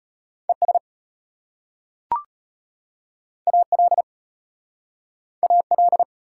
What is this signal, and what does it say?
Morse code, heard as a keyed tone, about 32 wpm, 690 Hz. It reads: ES AL AL